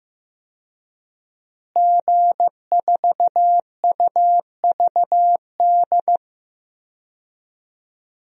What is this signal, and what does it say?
Morse code, heard as a keyed tone, about 15 wpm, 700 Hz. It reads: G4UVD
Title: DXCC-100 Call Signs - 5 Characters - Repeated 2x - 15wpm